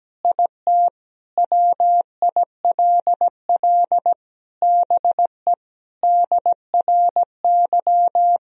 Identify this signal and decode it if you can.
Morse code, heard as a keyed tone, about 17 wpm, 700 Hz. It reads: IT WILL BE DRY
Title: Sentences from Top 1000 Words - Review Part 3 - 17wpm